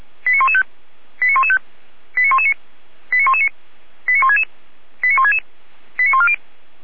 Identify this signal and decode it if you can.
Five-tone selective call (ZVEI), heard as quick runs of five tones, tones of about 70 ms. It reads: 89115 89115 89119 89119 89166 89166 89140